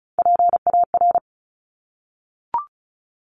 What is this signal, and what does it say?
Morse code, heard as a keyed tone, about 36 wpm, 700 Hz. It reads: PAR
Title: Top Three Letter Combinations in Words 35wpm